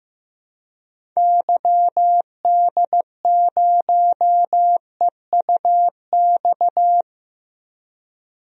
Morse code, heard as a keyed tone, about 15 wpm, 700 Hz. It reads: YD0EUX
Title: DXCC-100 Call Signs - 6 Characters - Repeated 3x - 15wpm